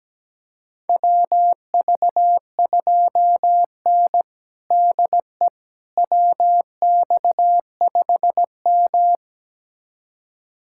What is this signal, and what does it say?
Morse code, heard as a keyed tone, about 17 wpm, 700 Hz. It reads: WV2N DE WX5M
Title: USA Callsign DE Callsign - 4 Characters - 17wpm